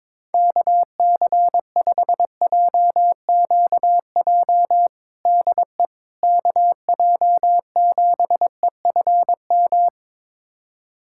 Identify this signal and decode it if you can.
Morse code, heard as a keyed tone, about 22 wpm, 700 Hz. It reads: KC5JQJ DE KJ7EFM
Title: USA Callsign DE Callsign - 6 Characters - 22wpm